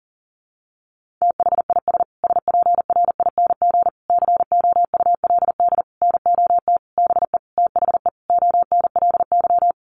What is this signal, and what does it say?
Morse code, heard as a keyed tone, about 40 wpm, 700 Hz. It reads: THIS SPRING COULD NOT BE THE ONLY
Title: Sentences from Top 1500 Words 40wpm